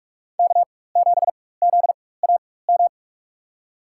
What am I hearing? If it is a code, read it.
Morse code, heard as a keyed tone, about 44 wpm, spaced as for 21 wpm, 700 Hz. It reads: K7ZAM
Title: USA Call Signs - 5 Characters - 4x Character Spacing - 45wpm